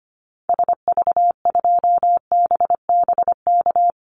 Morse code, heard as a keyed tone, about 25 wpm, 700 Hz. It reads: SV2BBK
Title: Call Signs Repeated 2x - 25wpm